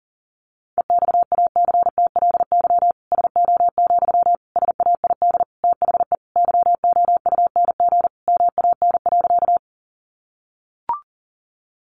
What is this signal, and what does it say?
Morse code, heard as a keyed tone, about 40 wpm, 700 Hz. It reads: EXACTLY SO, SAID THE YOUNG MAN.
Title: Sentences from Top 900 Words 40wpm